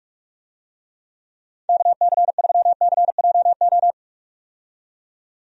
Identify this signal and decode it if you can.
Morse code, heard as a keyed tone, about 43 wpm, 700 Hz. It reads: KC3CJO
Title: USA Call Signs - 6 Characters - 45wpm